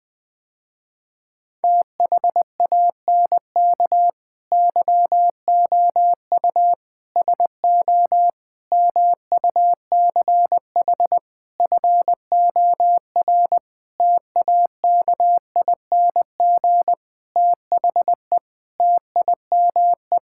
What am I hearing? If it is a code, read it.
Morse code, heard as a keyed tone, about 20 wpm, 700 Hz. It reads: THANK YOU SO MUCH FOR TAKING THE TIME
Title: Sentences with 9-Word Phrases - Spoken Beginning - 20wpm